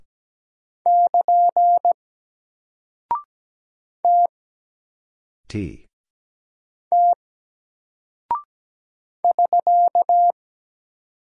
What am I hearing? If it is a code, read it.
Morse code, heard as a keyed tone, about 17 wpm, 700 Hz. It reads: <KN> T T <SK>
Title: Single Character (Letter:Number:Punctuation:Prosign) 17wpm